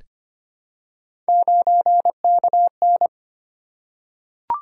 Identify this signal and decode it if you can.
Morse code, heard as a keyed tone, about 25 wpm, 700 Hz. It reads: 9KN
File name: Three Letter-Number Combinations - 25wpm